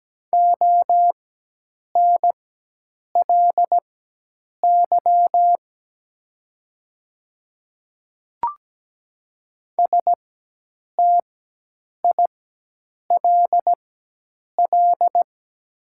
Morse code, heard as a keyed tone, about 17 wpm, spaced as for 8 wpm, 700 Hz. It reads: ONLY STILL
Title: 4x Character Spacing - Top 300 Words - 17wpm